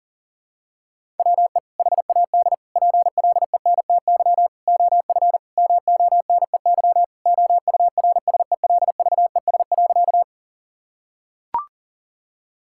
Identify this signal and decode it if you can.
Morse code, heard as a keyed tone, about 39 wpm, 700 Hz. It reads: WE HAD PLENTY OF MONEY OURSELVES.
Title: Sentences from Top 1600 Words 40wpm